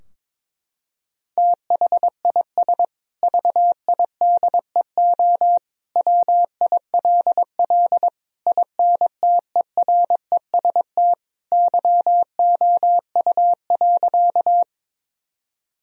Morse code, heard as a keyed tone, about 22 wpm, 700 Hz. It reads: THIS VIDEO WILL INTEREST YOU.
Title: Sentences from Top 1000 Words - Review Part 4 - 22wpm